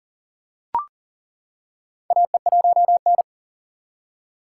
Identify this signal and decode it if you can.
Morse code, heard as a keyed tone, about 37 wpm, 700 Hz. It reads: AE1N
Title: K1USN SST QSO Practice - 40wpm